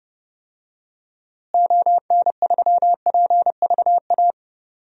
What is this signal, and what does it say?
Morse code, heard as a keyed tone, about 30 wpm, 700 Hz. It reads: ON3PVA
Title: DXCC-100 Call Signs - 6 Characters - Repeated 2x - 30wpm